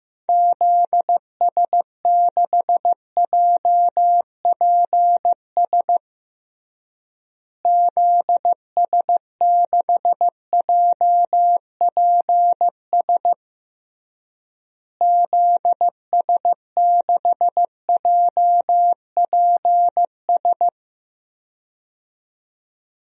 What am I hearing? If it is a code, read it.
Morse code, heard as a keyed tone, about 15 wpm, 700 Hz. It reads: ZS6JPS ZS6JPS ZS6JPS
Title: DXCC-100 Call Signs - 6 Characters - Repeated 3x - 15wpm